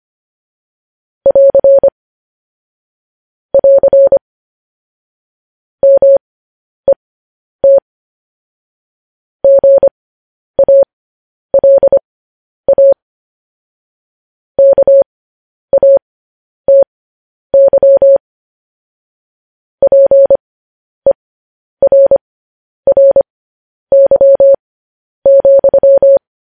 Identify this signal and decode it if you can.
Morse code, heard as a keyed tone, about 25 wpm, spaced as for 10 wpm, 550 Hz. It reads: <AR> <AR> MET GALA KATY PERRY,